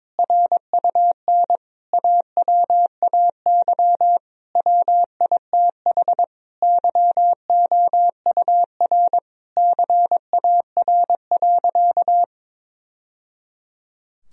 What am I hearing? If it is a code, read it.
Morse code, heard as a keyed tone, about 22 wpm, 700 Hz. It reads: RUN AWAY WITH YOUR CAR.
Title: Sentences from Top 1000 Words - Review Part 1 - 22wpm